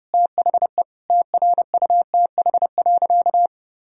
Morse code, heard as a keyed tone, about 30 wpm, 700 Hz. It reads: THE TRUTH.
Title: Sentences from Top 1000 Words - Review Part 4 - 30wpm